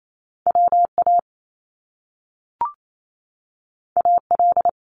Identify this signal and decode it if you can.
Morse code, heard as a keyed tone, about 28 wpm, 700 Hz. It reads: WA AL
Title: US States Repeated 3x 28wpm